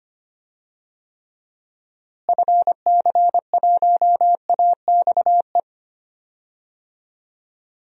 Morse code, heard as a keyed tone, about 25 wpm, 700 Hz. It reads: FC1AXE
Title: DXCC-100 Call Signs - 6 Characters - Repeated 3x - 25wpm